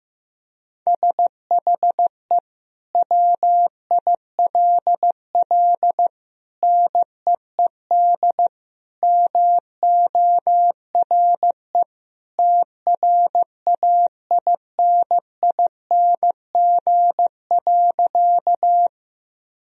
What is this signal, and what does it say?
Morse code, heard as a keyed tone, about 15 wpm, 700 Hz. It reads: SHE WILL NEED MORE TRAINING.
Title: Sentences from Top 1000 Words - Review Part 4 - 15wpm